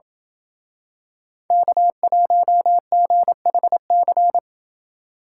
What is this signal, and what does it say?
Morse code, heard as a keyed tone, about 27 wpm, 700 Hz. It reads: K1GHC